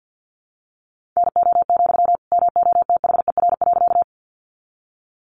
Morse code, heard as a keyed tone, about 51 wpm, 700 Hz. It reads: NO, MOTHER.
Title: Sentences from Top 200 Words 50wpm